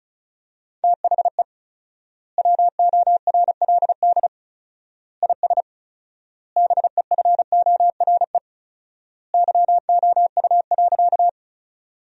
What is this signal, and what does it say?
Morse code, heard as a keyed tone, about 36 wpm, 700 Hz. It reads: THE WORLD IS BEFORE YOU.
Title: 4x Word-Spacing - Sentences from Top 200 Words - 35wpm